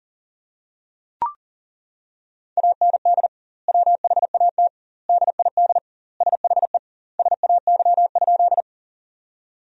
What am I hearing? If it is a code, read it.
Morse code, heard as a keyed tone, about 40 wpm, 700 Hz. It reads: AND WHAT DID SHE SAY?
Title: 2x Word-Spacing - Sentences from Top 200 Words - 40wpm